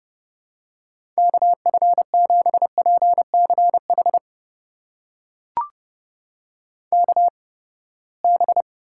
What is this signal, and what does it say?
Morse code, heard as a keyed tone, about 32 wpm, spaced as for 21 wpm, 700 Hz. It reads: KF7PCH KB
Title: USA Call Signs - 6 Characters - 8x Character Spacing - 30wpm